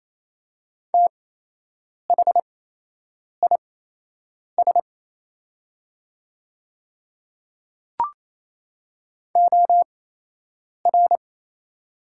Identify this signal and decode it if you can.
Morse code, heard as a keyed tone, about 28 wpm, spaced as for 8 wpm, 700 Hz. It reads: THIS OR
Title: 8x Character Spacing - Top 100 Words - 28wpm